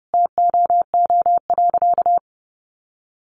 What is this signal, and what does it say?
Morse code, heard as a keyed tone, about 30 wpm, 700 Hz. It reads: TOO.
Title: Sentences from Top 1500 Words 30wpm